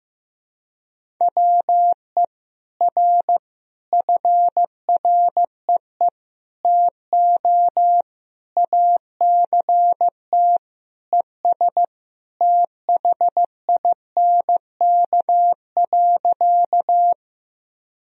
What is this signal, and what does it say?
Morse code, heard as a keyed tone, about 16 wpm, 700 Hz. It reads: WE R FREE TO ACT ES THINK.
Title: Sentences from Top 500 Words and Abbreviations 15wpm